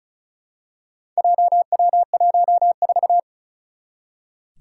Morse code, heard as a keyed tone, about 36 wpm, 700 Hz. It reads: JW14
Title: Four Challenging Letter-Number Combinations 35wpm